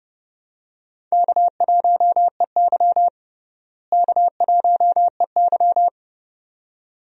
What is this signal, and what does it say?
Morse code, heard as a keyed tone, about 30 wpm, 700 Hz. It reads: K1EY K1EY
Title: USA Call Signs - 4 Characters - Repeated 2x - 30wpm